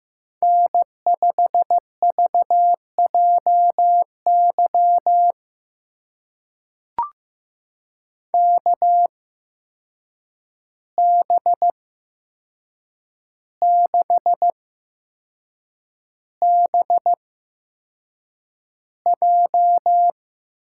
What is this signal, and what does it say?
Morse code, heard as a keyed tone, about 15 wpm, spaced as for 6 wpm, 700 Hz. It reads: N5VJY KB6BJ
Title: USA Call Signs - 5 Characters - 8x Character Spacing - 15wpm